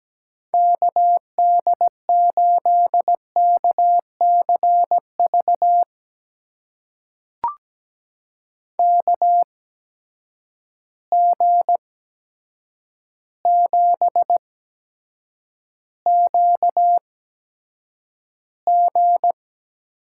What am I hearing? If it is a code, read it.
Morse code, heard as a keyed tone, about 17 wpm, spaced as for 8 wpm, 700 Hz. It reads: KD8KCV KG7QG
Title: USA Call Signs - 6 Characters - 8x Character Spacing - 17wpm